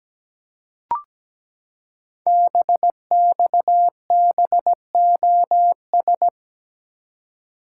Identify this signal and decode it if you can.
Morse code, heard as a keyed tone, about 17 wpm, 700 Hz. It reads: BXBOS
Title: Code Groups - 5 Letters - 17wpm